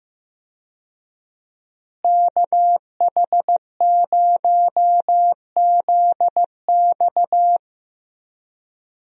Morse code, heard as a keyed tone, about 16 wpm, 700 Hz. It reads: KH0ZX
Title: USA Call Signs - 5 Characters - Repeated 3x - 15wpm